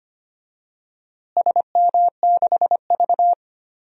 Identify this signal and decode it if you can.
Morse code, heard as a keyed tone, about 25 wpm, 700 Hz. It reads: SM6V
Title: DXCC-100 Call Signs - 4 Characters - Repeated 3x - 25wpm